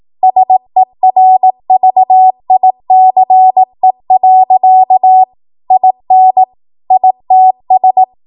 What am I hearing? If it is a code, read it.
Morse code, heard as a keyed tone, about 18 wpm, 750 Hz. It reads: SERVICE. IN ITS
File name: Morse / 18WPM